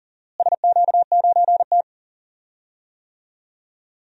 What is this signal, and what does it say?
Morse code, heard as a keyed tone, about 40 wpm, 700 Hz. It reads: SQ9T
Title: DXCC-100 Call Signs - 4 Characters - Repeated 3x - 40wpm